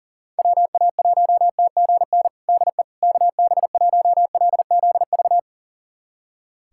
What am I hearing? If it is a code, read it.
Morse code, heard as a keyed tone, about 40 wpm, 700 Hz. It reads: WA1TGN DE KB1LZV
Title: USA Callsign DE Callsign - 6 Characters - 40wpm